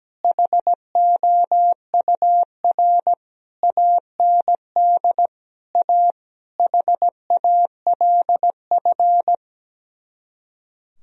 Morse code, heard as a keyed tone, about 17 wpm, 700 Hz. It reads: HOUR AND A HALF
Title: Speed-Racing - Sets of 5 Words 17wpm